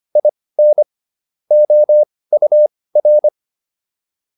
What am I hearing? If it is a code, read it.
Morse code, heard as a keyed tone, about 25 wpm, spaced as for 18 wpm, 600 Hz. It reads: IN OUR
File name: WD104 18